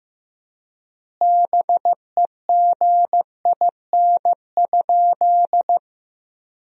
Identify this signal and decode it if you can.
Morse code, heard as a keyed tone, about 16 wpm, 700 Hz. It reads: BEGIN?
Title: 3x Word-Spacing - Sentences from Top 300 Words - 15wpm